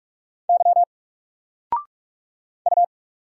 Morse code, heard as a keyed tone, about 44 wpm, 700 Hz. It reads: Y U
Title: Single Letter-Number 45wpm